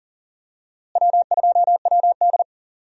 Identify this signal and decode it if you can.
Morse code, heard as a keyed tone, about 40 wpm, 700 Hz. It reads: W2WD
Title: USA Call Signs - 4 Characters - Repeated 2x - 40wpm